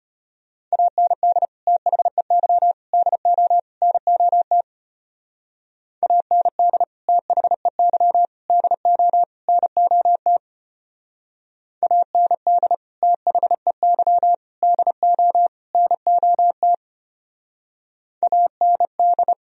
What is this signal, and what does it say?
Morse code, heard as a keyed tone, about 32 wpm, 700 Hz. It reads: AND THEY DO NOT AND THEY DO NOT AND THEY DO NOT AND